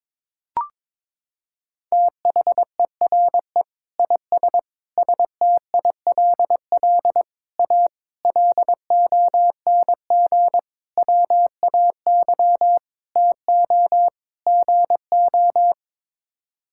Morse code, heard as a keyed tone, about 23 wpm, 700 Hz. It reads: THERE IS STILL A LONG WAY TO GO
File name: practice / 22wpm / Sets of 8 Words - 22wpm